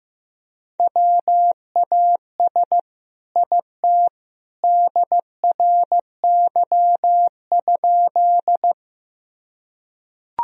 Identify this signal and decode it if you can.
Morse code, heard as a keyed tone, about 15 wpm, 700 Hz. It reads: WAS IT DRY?
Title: Sentences from Top 1000 Words - Review Part 3 - 15wpm